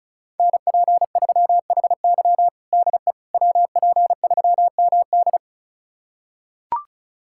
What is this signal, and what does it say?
Morse code, heard as a keyed tone, about 36 wpm, 700 Hz. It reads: NP3HY DE WP3MD
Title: USA Callsign DE Callsign - 5 Characters - 35wpm